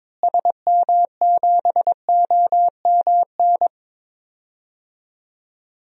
Morse code, heard as a keyed tone, about 22 wpm, 700 Hz. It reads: SM7OMN
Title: DXCC-100 Call Signs - 6 Characters - Repeated 3x - 22wpm